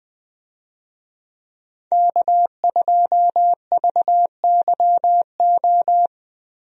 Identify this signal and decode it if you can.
Morse code, heard as a keyed tone, about 21 wpm, 700 Hz. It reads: K2VYO
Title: USA Call Signs - 5 Characters - Repeated 3x - 20wpm